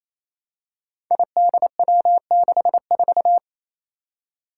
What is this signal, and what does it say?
Morse code, heard as a keyed tone, about 28 wpm, 700 Hz. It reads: IDW64
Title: Five Challenging Letter-Number Combinations 28wpm